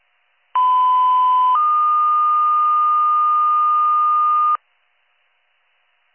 Two-tone page, heard as a one-second tone, then a three-second tone, about 1010 then 1220 Hz. It is heard from 0.5 to 4.6 s.